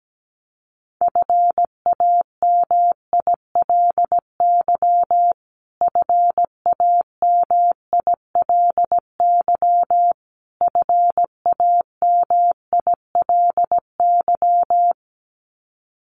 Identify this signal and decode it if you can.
Morse code, heard as a keyed tone, about 17 wpm, 700 Hz. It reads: FAMILY FAMILY FAMILY
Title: Top 500 Words Repeated 17wpm